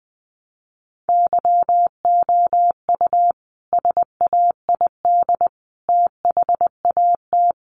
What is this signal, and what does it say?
Morse code, heard as a keyed tone, about 20 wpm, 700 Hz. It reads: YOU SAID THAT
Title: Sentences from Top 1300 Words 20wpm